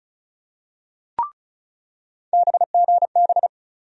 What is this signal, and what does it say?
Morse code, heard as a keyed tone, about 34 wpm, 700 Hz. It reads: BGB
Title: USA Call Sign Postfixes - 3 Letters 35wpm